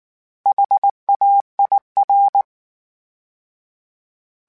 Morse code, heard as a keyed tone, about 19 wpm, 800 Hz. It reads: HAIR
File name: H Prefixes / H mp3s